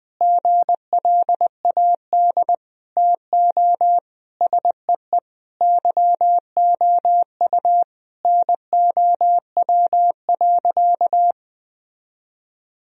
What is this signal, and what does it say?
Morse code, heard as a keyed tone, about 19 wpm, 700 Hz. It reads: GLAD TO SEE YOU NOW.
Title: Sentences from Top 1000 Words - Review Part 2 - 20wpm